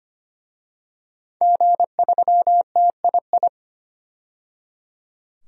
Morse code, heard as a keyed tone, about 25 wpm, 700 Hz. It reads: G3TII
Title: DXCC-100 Call Signs - 5 Characters - 25wpm